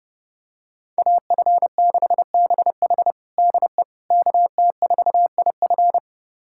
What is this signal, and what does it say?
Morse code, heard as a keyed tone, about 32 wpm, 700 Hz. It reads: AF6BH DE KT4IF
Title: USA Callsign DE Callsign - 5 Characters - 30wpm